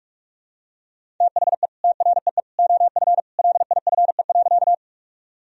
Morse code, heard as a keyed tone, about 41 wpm, 700 Hz. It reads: THE TREE OF LIFE.
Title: Sentences from Top 1000 Words - Review Part 1 - 45wpm